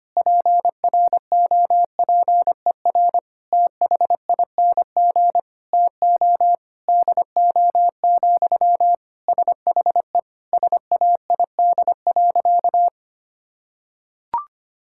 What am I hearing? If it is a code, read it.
Morse code, heard as a keyed tone, about 25 wpm, 700 Hz. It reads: PROPER THING TO DO, SHE SAID.
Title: Sentences from Top 1000 Words - Review Part 5 - 25wpm